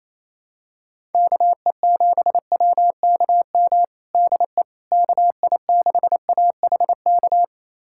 Morse code, heard as a keyed tone, about 28 wpm, 700 Hz. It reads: KE7WKM DE KI6AHK
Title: USA Callsign DE Callsign - 6 Characters - 28wpm